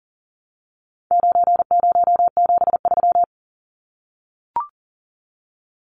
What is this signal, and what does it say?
Morse code, heard as a keyed tone, about 38 wpm, 700 Hz. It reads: 9073
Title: Numbers - 4-Digit 40wpm